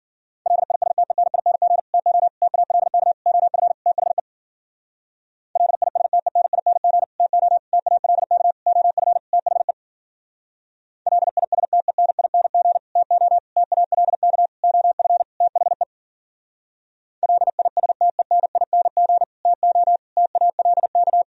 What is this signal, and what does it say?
Morse code, heard as a keyed tone, about 52 wpm, 700 Hz. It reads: LISTENING TO TALK OF THE LISTENING TO TALK OF THE LISTENING TO TALK OF THE LISTENING TO TALK